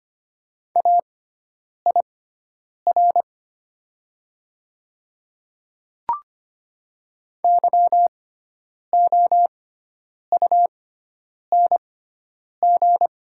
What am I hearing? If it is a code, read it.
Morse code, heard as a keyed tone, about 25 wpm, spaced as for 9 wpm, 700 Hz. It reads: AIR YOUNG
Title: 6x Character Spacing - Top 500 Words - 25wpm